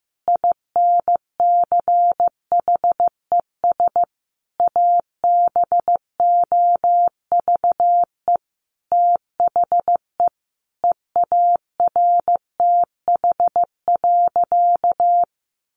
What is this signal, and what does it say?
Morse code, heard as a keyed tone, about 15 wpm, 700 Hz. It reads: INCHES ABOVE THE EARTH.
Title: A Sound Of Thunder 15wpm